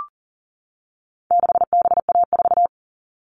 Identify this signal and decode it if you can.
Morse code, heard as a keyed tone, about 40 wpm, 700 Hz. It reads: 6BA4
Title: All Cut Numbers - 4-Digit 40wpm